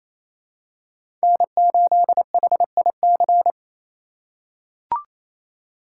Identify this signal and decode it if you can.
Morse code, heard as a keyed tone, about 28 wpm, 700 Hz. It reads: N8HIC
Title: USA Call Signs - 5 Characters - 6x Character Spacing - 28wpm